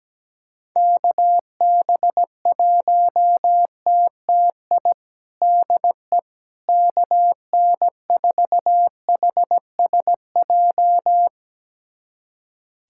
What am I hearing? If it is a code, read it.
Morse code, heard as a keyed tone, about 17 wpm, 700 Hz. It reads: KB1TTI DE KN4HSJ
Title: USA Callsign DE Callsign 17wpm